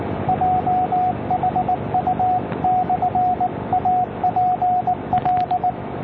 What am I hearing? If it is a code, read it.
Morse code, heard as a keyed tone, about 19 wpm, 730 Hz, with radio noise. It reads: JHU/APL